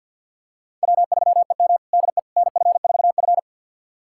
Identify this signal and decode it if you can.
Morse code, heard as a keyed tone, about 51 wpm, 700 Hz. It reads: W3EM DE NF4F
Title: USA Callsign DE Callsign - 4 Characters - 50wpm